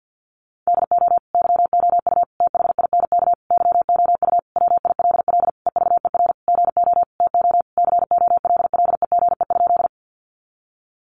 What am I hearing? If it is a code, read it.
Morse code, heard as a keyed tone, about 48 wpm, 700 Hz. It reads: DO YOU THINK YOU WILL EVER GO TO COLLEGE?
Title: Sentences from Top 1300 Words 50wpm